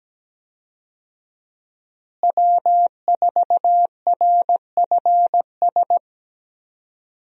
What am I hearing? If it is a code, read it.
Morse code, heard as a keyed tone, about 17 wpm, 700 Hz. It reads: W4RFS
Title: USA Call Signs - 5 Characters - Repeated 2x - 17wpm